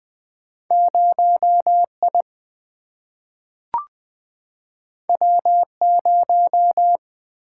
Morse code, heard as a keyed tone, about 20 wpm, 700 Hz. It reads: 0I W0
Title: Single-Letter Single-Number 20wpm